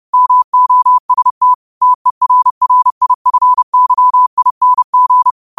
Morse code, heard as a keyed tone, about 30 wpm, 1000 Hz. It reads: MOST TERRIFYING